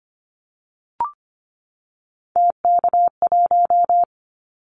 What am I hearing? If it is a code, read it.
Morse code, heard as a keyed tone, about 25 wpm, 700 Hz. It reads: TK1
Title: Callsign Prefixes for Top 100 DXCC - 25wpm